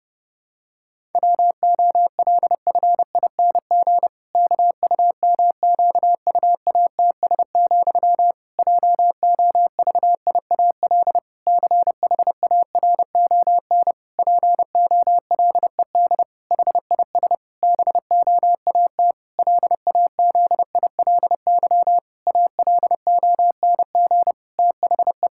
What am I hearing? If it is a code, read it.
Morse code, heard as a keyed tone, about 28 wpm, 700 Hz. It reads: WOLFING KUMQUATS, JOVIAL CHARON POLED HIS BOAT LAZILY ALONG THE
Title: Pangram Challenge v2 - 30wpm